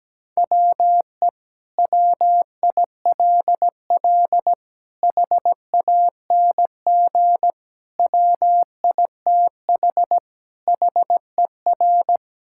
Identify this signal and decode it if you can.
Morse code, heard as a keyed tone, about 17 wpm, 700 Hz. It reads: WE WILL HANG WITH HER
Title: Sentences from Top 1600 Words 17wpm